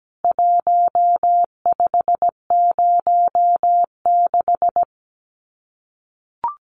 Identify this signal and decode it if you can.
Morse code, heard as a keyed tone, about 17 wpm, 700 Hz. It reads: 1506
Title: CWT Exchange 17wpm